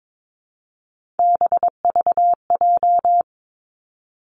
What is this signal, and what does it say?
Morse code, heard as a keyed tone, about 22 wpm, 700 Hz. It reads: BVJ
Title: Three Challenging Letter-Number Combinations 22wpm